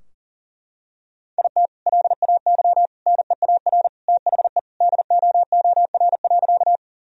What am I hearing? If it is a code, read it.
Morse code, heard as a keyed tone, about 37 wpm, 700 Hz. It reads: IT LAY NEAR THE DOOR.
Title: Sentences from Top 1000 Words - Review Part 2 - 40wpm